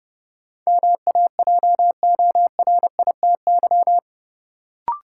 Morse code, heard as a keyed tone, about 32 wpm, 700 Hz. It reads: MAJORITY
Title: Sets of 3 Words - New Word Drill - 30wpm